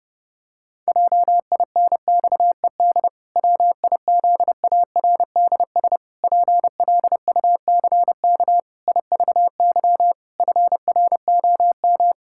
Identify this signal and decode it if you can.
Morse code, heard as a keyed tone, about 30 wpm, 700 Hz. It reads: JINXED WIZARDS PLUCK IVY FROM
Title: Pangram Challenge v2 - 30wpm